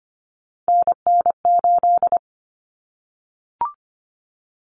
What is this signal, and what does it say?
Morse code, heard as a keyed tone, about 25 wpm, 700 Hz. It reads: NN8
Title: Common Cut Numbers - 3-Digit 25wpm